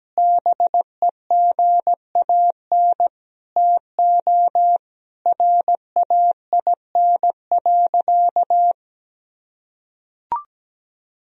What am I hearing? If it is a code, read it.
Morse code, heard as a keyed tone, about 17 wpm, 700 Hz. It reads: BEGAN TO RAIN.
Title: Sentences from Top 1000 Words - Review Part 1 - 17wpm